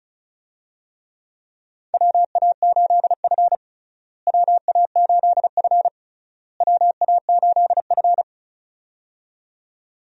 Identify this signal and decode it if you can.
Morse code, heard as a keyed tone, about 34 wpm, 700 Hz. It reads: WA8F WA8F WA8F
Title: USA Call Signs - 4 Characters - Repeated 3x - 35wpm